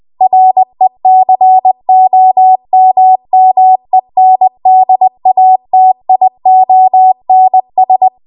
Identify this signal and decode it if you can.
Morse code, heard as a keyed tone, about 20 wpm, 750 Hz. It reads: RECOMMENDATIONS
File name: Archive / 20 WPM / 20WPM